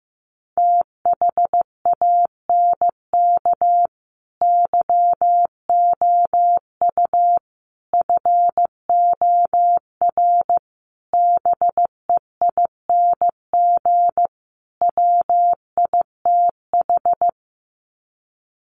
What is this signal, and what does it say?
Morse code, heard as a keyed tone, about 15 wpm, 700 Hz. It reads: THANK YOU FOR BEING WITH
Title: Sets of 5 Words 15wpm